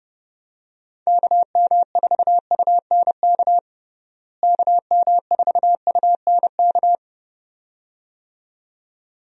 Morse code, heard as a keyed tone, about 31 wpm, 700 Hz. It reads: KM4UNK KM4UNK
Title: USA Call Signs - 6 Characters - Repeated 2x - 30wpm